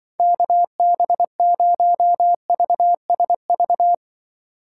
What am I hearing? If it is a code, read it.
Morse code, heard as a keyed tone, about 25 wpm, 700 Hz. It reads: KB0VSV
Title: Speed Racing - USA Call Signs 20wpm